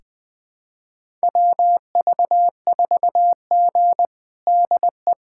Morse code, heard as a keyed tone, about 21 wpm, 700 Hz. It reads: WV4G DE
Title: USA Callsign DE Callsign - 4 Characters - 20wpm